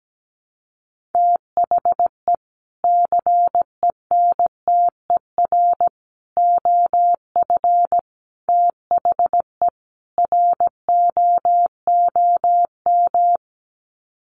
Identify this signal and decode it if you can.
Morse code, heard as a keyed tone, about 17 wpm, 700 Hz. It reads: THE CENTER OF THE ROOM
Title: Sets of 5 Words Encore 17wpm